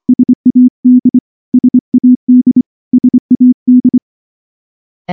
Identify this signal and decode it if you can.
Morse code, heard as a keyed tone, about 25 wpm, 250 Hz. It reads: SAD SAD SAD
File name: morse / morse-training-25WPM